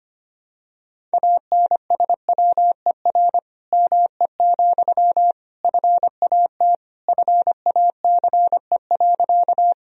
Morse code, heard as a keyed tone, about 25 wpm, 700 Hz. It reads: ANSWER ME, FAT FACE.
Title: Sentences from Top 1000 Words - Review Part 4 - 25wpm